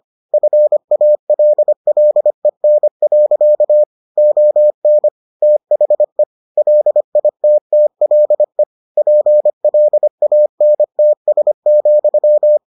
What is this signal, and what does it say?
Morse code, heard as a keyed tone, about 25 wpm, 600 Hz. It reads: FALLEN. ON THE LITTLE PLANTS,